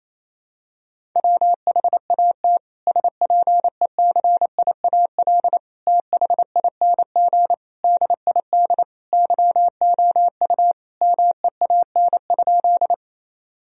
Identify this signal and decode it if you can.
Morse code, heard as a keyed tone, about 28 wpm, 700 Hz. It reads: WHAT SPECIAL THING DID YOU MEAN?
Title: Sentences from Top 1000 Words - Review Part 3 - 28wpm